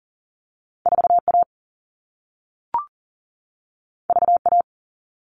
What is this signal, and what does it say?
Morse code, heard as a keyed tone, about 40 wpm, 700 Hz. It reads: 4A VA
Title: All Cut Numbers - 2-Digit 40wpm